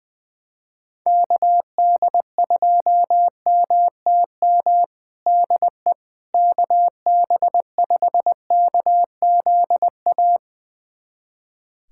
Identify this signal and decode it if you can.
Morse code, heard as a keyed tone, about 20 wpm, 700 Hz. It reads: KD2MTM DE KB5KZA
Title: USA Callsign DE Callsign - 6 Characters - 20wpm